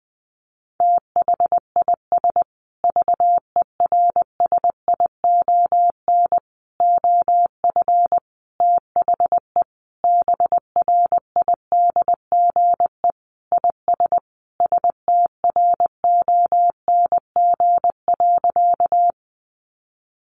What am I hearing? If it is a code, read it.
Morse code, heard as a keyed tone, about 20 wpm, 700 Hz. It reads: THIS VERSION OF THE BRIDGE IS STRONG.
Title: Sentences from Top 1400 Words 20wpm